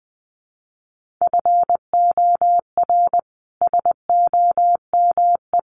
Morse code, heard as a keyed tone, about 19 wpm, 700 Hz. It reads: FOR SOME
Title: Sentences from Top 200 Words 20wpm